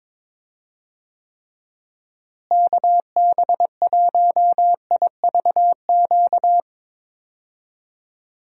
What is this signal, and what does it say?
Morse code, heard as a keyed tone, about 22 wpm, 700 Hz. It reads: KB1IVQ
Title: USA Call Signs - 6 Characters - Repeated 3x - 22wpm